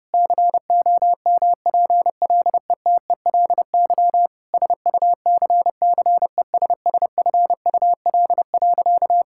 Morse code, heard as a keyed tone, about 30 wpm, 700 Hz. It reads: COMPLETELY SUCCESSFUL.
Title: Sentences from Top 1600 Words 30wpm